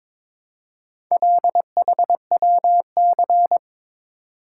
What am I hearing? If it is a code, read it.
Morse code, heard as a keyed tone, about 22 wpm, 700 Hz. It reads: LHWC
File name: practice / 22wpm / Four Letter-Number Combinations - 22wpm